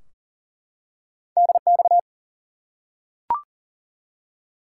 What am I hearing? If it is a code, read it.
Morse code, heard as a keyed tone, about 40 wpm, 700 Hz. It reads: DX
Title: Top 90 QSO Elements 40wpm